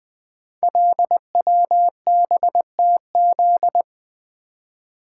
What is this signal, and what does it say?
Morse code, heard as a keyed tone, about 20 wpm, 700 Hz. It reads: LWBTZ
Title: Code Groups - 5 Letter:Numbers - 20wpm